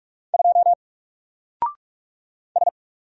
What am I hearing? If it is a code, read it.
Morse code, heard as a keyed tone, about 44 wpm, 700 Hz. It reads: 2 S
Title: Single Character (Letter:Number:Punctuation:Prosign) 45wpm